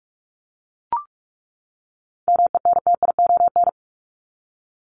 Morse code, heard as a keyed tone, about 44 wpm, 700 Hz. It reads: MENTION
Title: ED Suffix - Top 100 Words with Base Words 45wpm